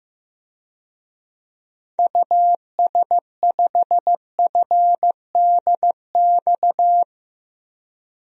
Morse code, heard as a keyed tone, about 15 wpm, 700 Hz. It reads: US5FDX
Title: DXCC-100 Call Signs - 6 Characters - Repeated 2x - 15wpm